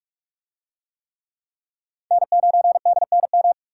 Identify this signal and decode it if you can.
Morse code, heard as a keyed tone, about 41 wpm, 700 Hz. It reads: N9DNM